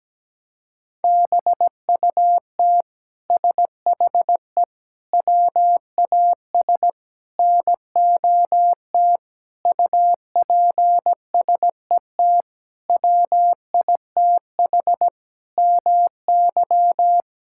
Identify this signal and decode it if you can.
Morse code, heard as a keyed tone, about 17 wpm, 700 Hz. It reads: BUT SHE WAS NOT UPSET WITH MY
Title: Sentences from Top 1000 Words - Review Part 5 - 17wpm